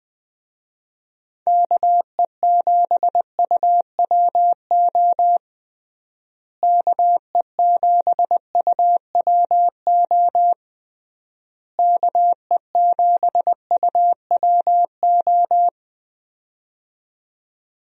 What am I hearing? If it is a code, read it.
Morse code, heard as a keyed tone, about 20 wpm, 700 Hz. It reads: KE7UWO KE7UWO KE7UWO
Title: USA Call Signs - 6 Characters - Repeated 3x - 20wpm